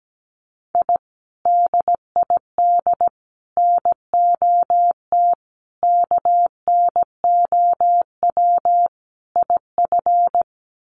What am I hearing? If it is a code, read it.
Morse code, heard as a keyed tone, about 17 wpm, 690 Hz. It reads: I DID NOT KNOW IF
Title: Sets of 5 Words Encore 17wpm